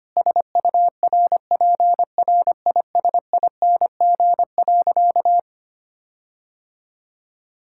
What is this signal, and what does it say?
Morse code, heard as a keyed tone, about 25 wpm, 700 Hz. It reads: SURPRISING.
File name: practice / 25wpm / Sentences with 4-Word Phrases - 25wpm